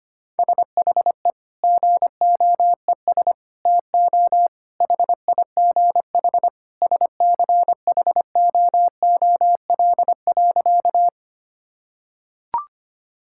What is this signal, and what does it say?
Morse code, heard as a keyed tone, about 25 wpm, 700 Hz. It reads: SHE GOES TO HIGH SCHOOL.
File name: practice / 25wpm / Sentences with 2-Word Phrases - 25wpm